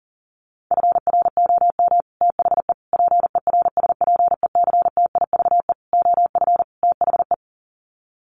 Morse code, heard as a keyed tone, about 40 wpm, 700 Hz. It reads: FROM THE PERSPECTIVE OF THE
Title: Sets of 5 Words Encore2 40wpm